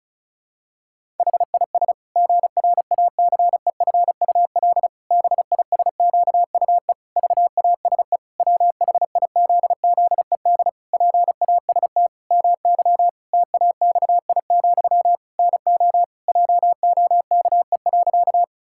Morse code, heard as a keyed tone, about 36 wpm, 700 Hz. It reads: HIS GRACEFUL BISQUE VASE WHIZZED PAST MY TAXI, NO JOKE.
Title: Pangram Challenge v2 - 35wpm